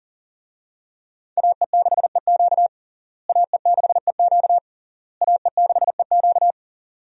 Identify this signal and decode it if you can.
Morse code, heard as a keyed tone, about 40 wpm, 700 Hz. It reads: AE6EQ AE6EQ AE6EQ
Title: USA Call Signs - 5 Characters - Repeated 3x - 40wpm